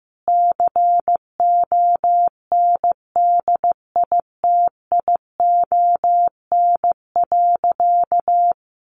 Morse code, heard as a keyed tone, about 15 wpm, 700 Hz. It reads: CONDITION.
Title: Sentences from Top 1000 Words 15wpm